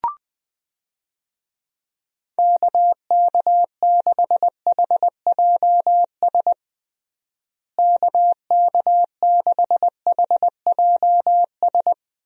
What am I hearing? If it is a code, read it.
Morse code, heard as a keyed tone, about 21 wpm, 700 Hz. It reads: KK6HJS KK6HJS
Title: USA Call Signs - 6 Characters - Repeated 2x - 20wpm